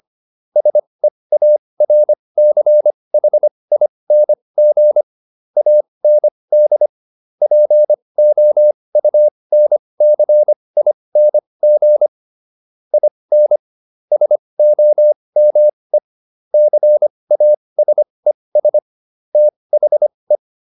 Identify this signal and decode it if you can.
Morse code, heard as a keyed tone, about 25 wpm, spaced as for 20 wpm, 600 Hz. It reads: SEARCHING AND POUNCING IN SOME CASES THE